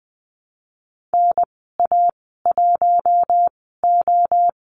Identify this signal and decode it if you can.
Morse code, heard as a keyed tone, about 20 wpm, spaced as for 15 wpm, 700 Hz. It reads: NA1O
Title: 076 Lesson 13g - Letter L - Call Signs 20wpm